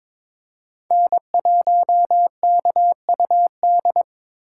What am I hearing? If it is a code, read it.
Morse code, heard as a keyed tone, about 22 wpm, 700 Hz. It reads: N1KUD
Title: USA Call Signs - 5 Characters - 3x Character Spacing - 22wpm